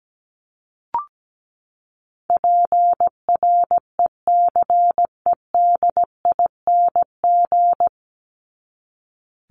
Morse code, heard as a keyed tone, about 17 wpm, 700 Hz. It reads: PRECEDING
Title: ING Suffix - Top 100 Words with Base Words 17wpm